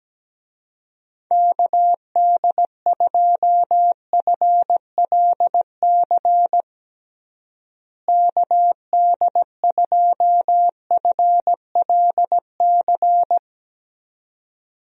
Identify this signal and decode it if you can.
Morse code, heard as a keyed tone, about 17 wpm, 700 Hz. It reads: KD2FLC KD2FLC
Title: USA Call Signs - 6 Characters - Repeated 2x - 17wpm